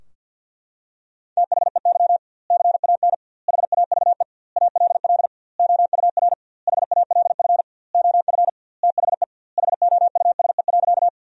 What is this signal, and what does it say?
Morse code, heard as a keyed tone, about 47 wpm, 700 Hz. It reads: THEY CAN HAVE ALL OUR HALF OF THE HOUSE.
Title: 2x Word-Spacing - Sentences from Top 400 Words - 50wpm